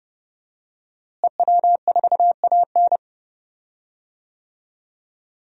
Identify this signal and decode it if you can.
Morse code, heard as a keyed tone, about 30 wpm, 700 Hz. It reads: EW4AN
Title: DXCC-100 Call Signs - 5 Characters - Repeated 3x - 30wpm